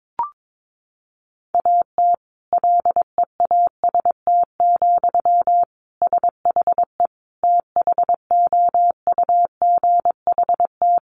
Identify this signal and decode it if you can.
Morse code, heard as a keyed tone, about 22 wpm, 700 Hz. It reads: AT LEAST, SHE THOUGHT
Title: Sentences from Top 1400 Words 22wpm